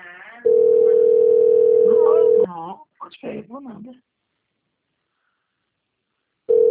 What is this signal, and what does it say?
Ringback tone (North America), 440 + 480 Hz, with speech.
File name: out- - - - - .32630